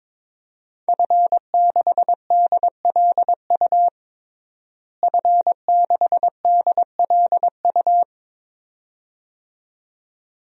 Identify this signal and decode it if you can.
Morse code, heard as a keyed tone, about 22 wpm, 700 Hz. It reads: F6DLU F6DLU
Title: DXCC-100 Call Signs - 5 Characters - Repeated 2x - 22wpm